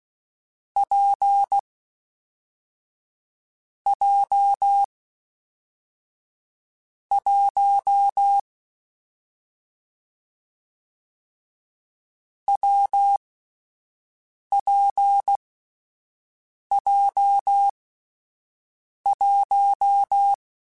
Morse code, heard as a keyed tone, about 16 wpm, spaced as for 5 wpm, 800 Hz. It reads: P J 1 WPJ1
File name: Difficult Combos / Difficult Combos-mp3s / WPJ WPJ1 @5